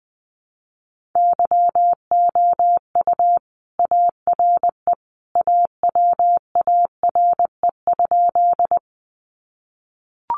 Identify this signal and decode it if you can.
Morse code, heard as a keyed tone, about 20 wpm, 700 Hz. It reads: YOU ARE AWARE?
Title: Sentences from Top 1200 Words 20wpm